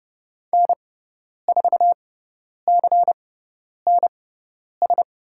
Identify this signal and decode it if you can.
Morse code, heard as a keyed tone, about 30 wpm, spaced as for 10 wpm, 700 Hz. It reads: N4CNS
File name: Lesson 16g - Letter c - Call Signs 30wpm